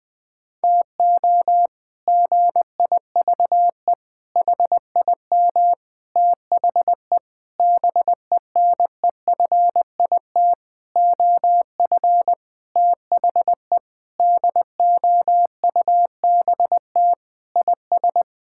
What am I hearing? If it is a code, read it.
Morse code, heard as a keyed tone, about 21 wpm, 700 Hz. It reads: TO GIVE HIM THE BENEFIT OF THE DOUBT IS
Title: Sentences with 8-Word Phrases - 20wpm